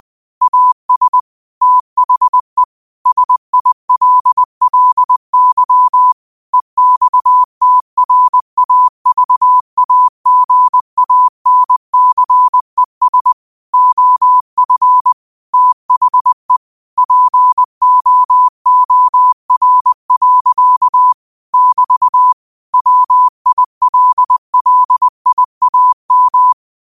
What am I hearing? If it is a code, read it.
Morse code, heard as a keyed tone, about 20 wpm, 1000 Hz. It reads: AS THE SILLY EXTRAVAGANCES OF THE POOR. <BT> WILLIAM
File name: qotdfiles / 20wpm